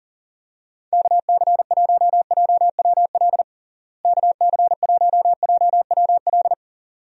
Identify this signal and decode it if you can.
Morse code, heard as a keyed tone, about 40 wpm, 700 Hz. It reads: KC1JWL KC1JWL
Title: USA Call Signs - 6 Characters - Repeated 2x - 40wpm